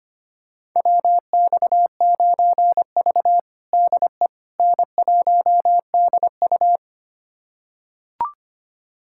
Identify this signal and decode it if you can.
Morse code, heard as a keyed tone, about 25 wpm, 700 Hz. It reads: WX9V DE N1DU
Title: USA Callsign DE Callsign - 4 Characters - 25wpm